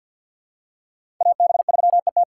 Morse code, heard as a keyed tone, about 46 wpm, 700 Hz. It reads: AB3ET
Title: USA Call Signs - 5 Characters - 6x Character Spacing - 50wpm